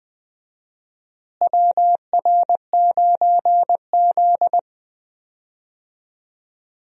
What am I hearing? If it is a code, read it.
Morse code, heard as a keyed tone, about 20 wpm, 700 Hz. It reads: WR9Z
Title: USA Call Signs - 4 Characters - Repeated 2x - 20wpm